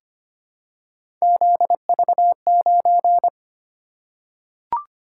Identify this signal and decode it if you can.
Morse code, heard as a keyed tone, about 25 wpm, 700 Hz. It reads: ZV9
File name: Three Letter-Number Combinations - 25wpm